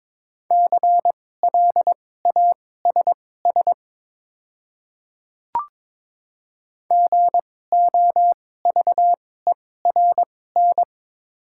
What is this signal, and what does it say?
Morse code, heard as a keyed tone, about 22 wpm, spaced as for 16 wpm, 700 Hz. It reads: CLASS GOVERN
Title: 2x Character Spacing - Top 500 Words - 22wpm